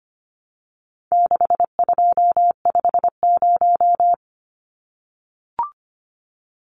Morse code, heard as a keyed tone, about 25 wpm, 700 Hz. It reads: 6250
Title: Numbers - 4-Digit 25wpm